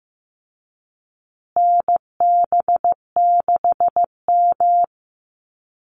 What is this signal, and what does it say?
Morse code, heard as a keyed tone, about 15 wpm, 700 Hz. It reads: NB6M
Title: Call Signs Repeated 2x - 15wpm